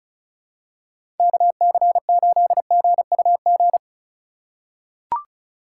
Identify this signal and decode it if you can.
Morse code, heard as a keyed tone, about 36 wpm, 700 Hz. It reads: KC8GUG
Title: USA Call Signs - 6 Characters - 6x Character Spacing - 35wpm